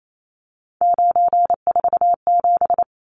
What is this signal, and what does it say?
Morse code, heard as a keyed tone, about 28 wpm, 700 Hz. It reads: 947
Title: Numbers - 3-Digit 28wpm